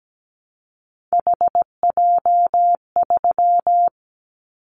Morse code, heard as a keyed tone, about 17 wpm, 700 Hz. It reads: HJ3
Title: Callsign Prefixes for Top 100 DXCC - 17wpm